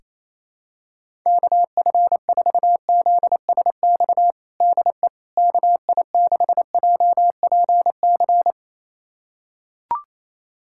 Morse code, heard as a keyed tone, about 28 wpm, 700 Hz. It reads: KF4ZSX DE KI6JPC
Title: USA Callsign DE Callsign 28wpm